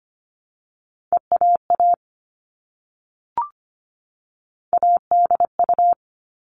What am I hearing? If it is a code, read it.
Morse code, heard as a keyed tone, about 25 wpm, 700 Hz. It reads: EAA ADU
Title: All Cut Numbers - 3-Digit 25wpm